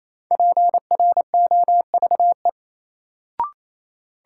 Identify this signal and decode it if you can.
Morse code, heard as a keyed tone, about 28 wpm, 700 Hz. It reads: PROVE
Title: Sentences from Top 1600 Words - New Word Drill - 28wpm